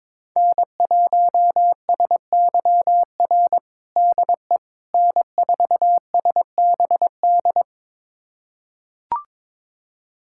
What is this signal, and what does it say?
Morse code, heard as a keyed tone, about 22 wpm, 700 Hz. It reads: N1SYR DE N4SBD
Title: USA Callsign DE Callsign - 5 Characters - 22wpm